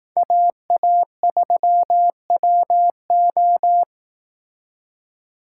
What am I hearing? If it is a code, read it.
Morse code, heard as a keyed tone, about 18 wpm, 700 Hz. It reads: AA3WO
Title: Speed Racing - USA 2x2 Call Signs 15wpm